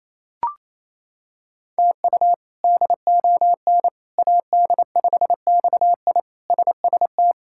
Morse code, heard as a keyed tone, about 28 wpm, 700 Hz. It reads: TU DON AD5XI SST
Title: K1USN SST QSO Practice - 28wpm